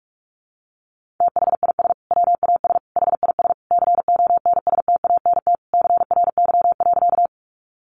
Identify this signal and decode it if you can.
Morse code, heard as a keyed tone, about 46 wpm, 700 Hz. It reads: THIS WAS HIS CONSTANT CRY.
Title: Sentences from Top 1500 Words 45wpm